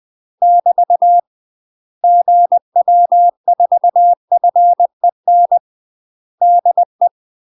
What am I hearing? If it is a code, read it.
Morse code, heard as a keyed tone, about 20 wpm, 700 Hz. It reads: <BT> GW4FEN DE
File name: UK MASTER QSOs TRAINING / 20WPM / UK MASTER QSOs 0026